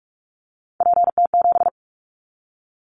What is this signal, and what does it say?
Morse code, heard as a keyed tone, about 44 wpm, 700 Hz. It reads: PT7
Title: Callsign Prefixes for Top 100 DXCC - 45wpm